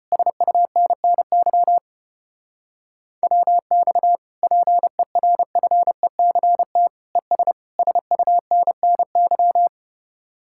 Morse code, heard as a keyed tone, about 33 wpm, 700 Hz. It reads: SUNNY WX PERFECT ES SUNNY